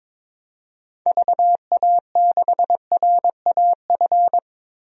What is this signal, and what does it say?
Morse code, heard as a keyed tone, about 22 wpm, 700 Hz. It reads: VA6RAF
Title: DXCC-100 Call Signs - 6 Characters - Repeated 2x - 22wpm